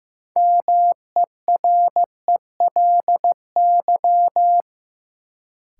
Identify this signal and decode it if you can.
Morse code, heard as a keyed tone, about 15 wpm, 700 Hz. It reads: MERELY
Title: Sentences from Top 1400 Words - New Word Drill - 15wpm